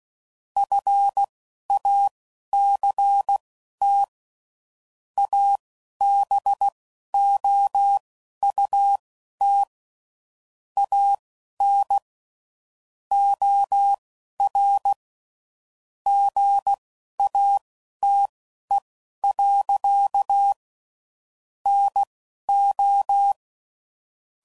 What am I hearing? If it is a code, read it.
Morse code, heard as a keyed tone, about 16 wpm, spaced as for 11 wpm, 800 Hz. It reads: FACT ABOUT AN OR GATE. NO
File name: CW Code Groups @10WPM / Technical Subjects / OR Gate